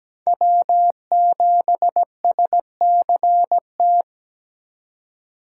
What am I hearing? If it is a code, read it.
Morse code, heard as a keyed tone, about 17 wpm, 700 Hz. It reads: W7SCT
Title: USA Call Signs - 5 Characters - 4x Character Spacing - 17wpm